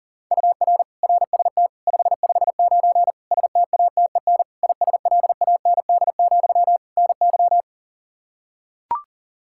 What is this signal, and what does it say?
Morse code, heard as a keyed tone, about 40 wpm, 700 Hz. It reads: UR RST 559 STATEN ISLAND, NY
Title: Practice QSO - Exchange 1 - Beginning - 40wpm